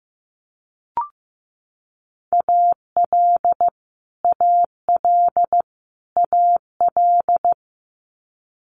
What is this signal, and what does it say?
Morse code, heard as a keyed tone, about 15 wpm, 700 Hz. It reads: AL AL AL
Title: Multi-letter Phonemes Repeated 3x 15wpm